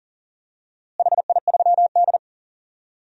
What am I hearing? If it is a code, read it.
Morse code, heard as a keyed tone, about 38 wpm, 700 Hz. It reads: HI3D
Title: DXCC-100 Call Signs - 4 Characters - Repeated 2x - 40wpm